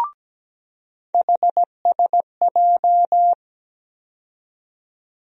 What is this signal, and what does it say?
Morse code, heard as a keyed tone, about 17 wpm, 700 Hz. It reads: HSJ
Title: USA Call Sign Postfixes - 3 Letters 17wpm